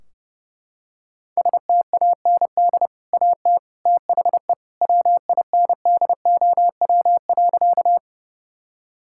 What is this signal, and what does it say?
Morse code, heard as a keyed tone, about 30 wpm, 700 Hz. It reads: STAND AT THE WINDOW.
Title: Sentences from Top 1600 Words 30wpm